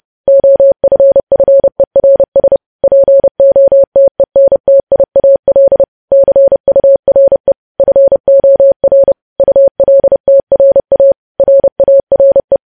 Morse code, heard as a keyed tone, about 30 wpm, 550 Hz. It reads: OFFERS POTENTIAL CURE FOR ULTRA RARE
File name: podcast / News headlines-1x-30x30